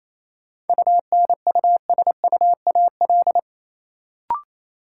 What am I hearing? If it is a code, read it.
Morse code, heard as a keyed tone, about 28 wpm, 700 Hz. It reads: UNUSUAL
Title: Sentences from Top 1400 Words - New Word Drill Alt - 28wpm